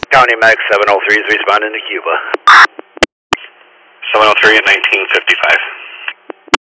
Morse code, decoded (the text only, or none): none